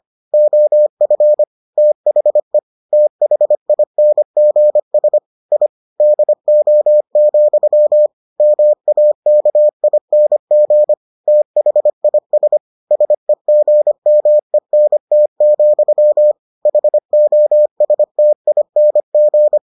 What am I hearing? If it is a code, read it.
Morse code, heard as a keyed tone, about 25 wpm, 600 Hz. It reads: OF THE THINGS I DO, MAKING THIS SEGMENT, HOSTING